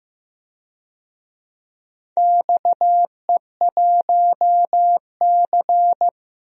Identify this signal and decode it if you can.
Morse code, heard as a keyed tone, about 15 wpm, 700 Hz. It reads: XE1C
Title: DXCC-100 Call Signs - 4 Characters - 15wpm